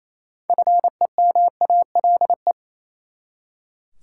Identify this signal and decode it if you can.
Morse code, heard as a keyed tone, about 28 wpm, 700 Hz. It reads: FEMALE
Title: Six Letter Words - 28wpm